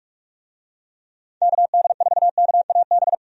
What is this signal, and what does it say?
Morse code, heard as a keyed tone, about 46 wpm, 700 Hz. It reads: KD4KAB
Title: USA Call Signs - 6 Characters - 45wpm